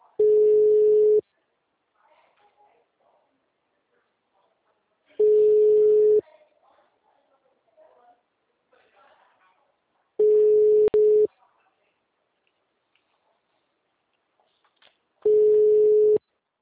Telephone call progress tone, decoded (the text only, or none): ringback tone